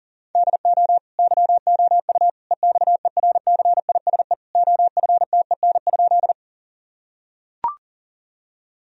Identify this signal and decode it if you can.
Morse code, heard as a keyed tone, about 37 wpm, 700 Hz. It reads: DO YOU EXERCISE OFTEN?
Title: Sentences from Top 1000 Words - Review Part 4 - 40wpm